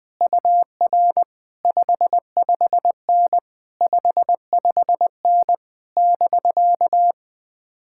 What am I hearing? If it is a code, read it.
Morse code, heard as a keyed tone, about 20 wpm, 700 Hz. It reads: UR 55N 55N <BK>
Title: Speed-Racing Common QSO Phrases - 17wpm